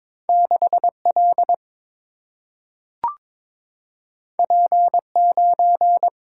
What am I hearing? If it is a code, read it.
Morse code, heard as a keyed tone, about 22 wpm, 700 Hz. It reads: 6 L P 9
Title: Single-Letter Single-Number 22wpm